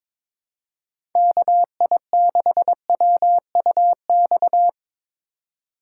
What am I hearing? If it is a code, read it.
Morse code, heard as a keyed tone, about 22 wpm, 700 Hz. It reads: KI6WUX
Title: USA Call Signs - 6 Characters - 4x Character Spacing - 22wpm